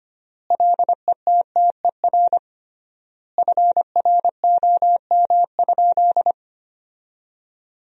Morse code, heard as a keyed tone, about 25 wpm, 700 Hz. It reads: LETTER FROM?
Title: 3x Word-Spacing - Sentences from Top 300 Words - 25wpm